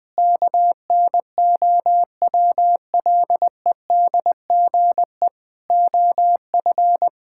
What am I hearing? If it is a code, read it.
Morse code, heard as a keyed tone, about 20 wpm, 700 Hz. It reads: KNOWLEDGE OF
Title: Sets of 8 Words - 20wpm